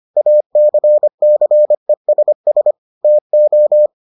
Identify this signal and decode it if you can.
Morse code, heard as a keyed tone, about 25 wpm, 600 Hz. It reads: ACCESS TO